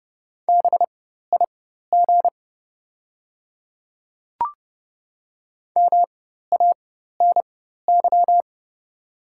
Morse code, heard as a keyed tone, about 30 wpm, spaced as for 14 wpm, 690 Hz. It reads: BIG MANY
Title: 4x Character Spacing - Top 200 Words - 30wpm